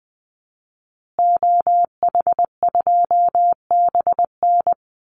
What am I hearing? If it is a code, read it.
Morse code, heard as a keyed tone, about 20 wpm, 700 Hz. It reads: OH2BN
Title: Call Signs Repeated 2x - 20wpm